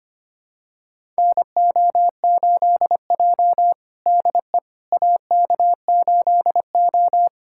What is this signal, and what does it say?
Morse code, heard as a keyed tone, about 25 wpm, 700 Hz. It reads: NO8J DE AK8O
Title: USA Callsign DE Callsign - 4 Characters - 25wpm